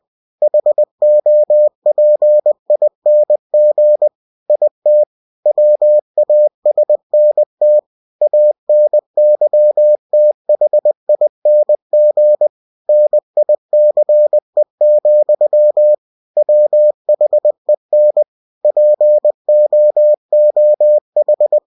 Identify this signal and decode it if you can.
Morse code, heard as a keyed tone, about 20 wpm, 600 Hz. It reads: HOPING IT WASNT ANYTHING NICE, WHEN POOH
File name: cw-books / winnie-the-pooh / winnie-20wpm-page-0078